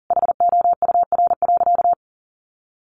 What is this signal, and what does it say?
Morse code, heard as a keyed tone, about 40 wpm, 700 Hz. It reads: HOUR.
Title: Sentences from Top 1500 Words 40wpm